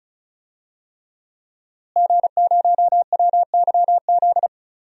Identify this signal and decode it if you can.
Morse code, heard as a keyed tone, about 34 wpm, 700 Hz. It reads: G0WYZ
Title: DXCC-100 Call Signs - 5 Characters - Repeated 3x - 35wpm